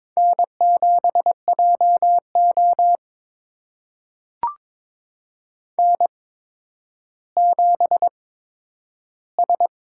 Morse code, heard as a keyed tone, about 22 wpm, spaced as for 11 wpm, 700 Hz. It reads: N7JO N7S
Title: USA Call Signs - 8x Character Spacing - 22wpm